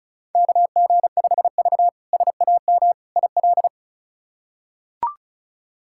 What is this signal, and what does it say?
Morse code, heard as a keyed tone, about 34 wpm, 700 Hz. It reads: KG5V SAM IL
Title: K1USN SST QSO Practice - 35wpm